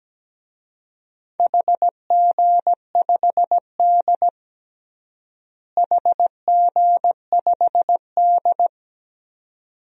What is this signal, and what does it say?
Morse code, heard as a keyed tone, about 17 wpm, 700 Hz. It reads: HG5D HG5D
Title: DXCC-100 Call Signs - 4 Characters - Repeated 2x - 17wpm